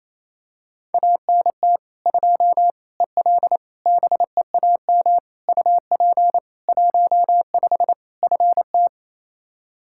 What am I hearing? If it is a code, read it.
Morse code, heard as a keyed tone, about 28 wpm, 700 Hz. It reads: ANT 2 EL BEAM UP 15 FT
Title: Practice QSO - Exchange 2 - Info - 28wpm